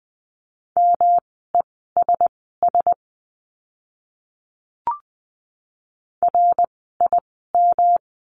Morse code, heard as a keyed tone, about 20 wpm, spaced as for 14 wpm, 700 Hz. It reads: MESS RIM
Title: 108 Lesson 18f - Letter m - Words 20wpm